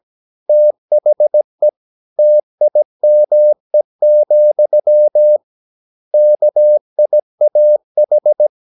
Morse code, heard as a keyed tone, about 17 wpm, 600 Hz. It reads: THE TIME, KIAH
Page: Romanen Never af Ken Follet på engelsk.